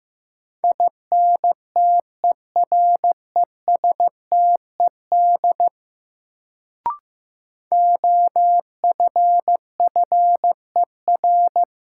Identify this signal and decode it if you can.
Morse code, heard as a keyed tone, about 15 wpm, 700 Hz. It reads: INTERESTED OFFER
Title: Sentences from Top 900 Words - New Word Drill - 15wpm